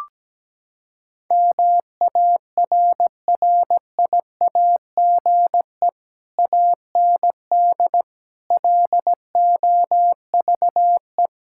Morse code, heard as a keyed tone, about 17 wpm, 700 Hz. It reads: MARRIAGE AND LOVE
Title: Sentences from Top 1000 Words - Review Part 5 - 17wpm